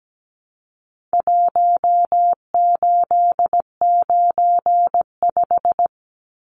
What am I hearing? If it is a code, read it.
Morse code, heard as a keyed tone, about 17 wpm, 700 Hz. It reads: 1895
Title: Numbers - Contest Style - 4-Digit 17wpm